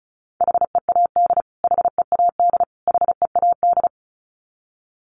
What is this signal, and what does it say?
Morse code, heard as a keyed tone, about 34 wpm, 700 Hz. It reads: HEAD HEAD HEAD
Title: Top 300 Words Repeated 35wpm